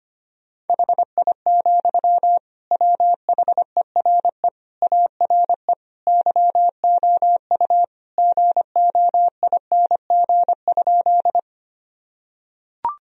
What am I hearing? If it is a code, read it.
Morse code, heard as a keyed tone, about 25 wpm, 700 Hz. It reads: HI, WHERE ARE YOU GOING?
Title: Sentences from Top 1000 Words - Review Part 3 - 25wpm